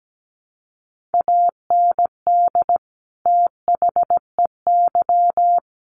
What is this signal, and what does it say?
Morse code, heard as a keyed tone, about 17 wpm, 700 Hz. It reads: AND THEY
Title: Sentences from Top 200 Words 17wpm